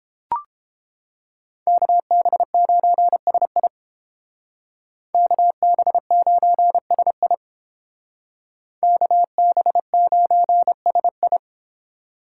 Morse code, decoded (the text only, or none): KB9SI KB9SI KB9SI